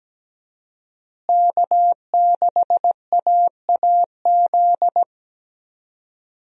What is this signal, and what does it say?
Morse code, heard as a keyed tone, about 17 wpm, 700 Hz. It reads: K6AAZ
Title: USA Call Signs - 5 Characters - Repeated 3x - 17wpm